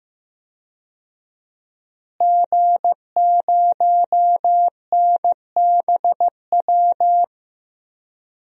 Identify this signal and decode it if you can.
Morse code, heard as a keyed tone, about 15 wpm, 700 Hz. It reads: G0NBW
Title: DXCC-100 Call Signs - 5 Characters - Repeated 3x - 15wpm